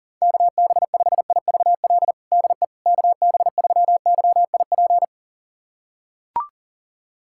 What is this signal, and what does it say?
Morse code, heard as a keyed tone, about 37 wpm, 700 Hz. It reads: KB5IVL DE KB3YIP
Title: USA Callsign DE Callsign - 6 Characters - 40wpm